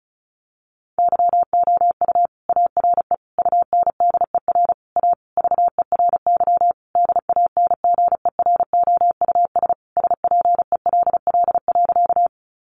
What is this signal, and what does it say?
Morse code, heard as a keyed tone, about 34 wpm, 700 Hz. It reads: YOU ARE UNDER A VERY DANGEROUS SPELL.
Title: Sentences from Top 1200 Words 35wpm